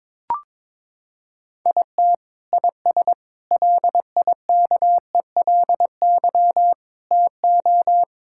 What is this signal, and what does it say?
Morse code, heard as a keyed tone, about 23 wpm, 700 Hz. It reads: IT IS LIKELY TO
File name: practice / 22wpm / Sentences with 2-Word Phrases - 22wpm